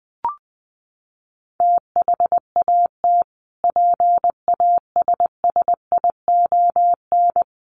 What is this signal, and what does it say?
Morse code, heard as a keyed tone, about 20 wpm, 700 Hz. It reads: THAT PASSION
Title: Sentences from Top 1300 Words 20wpm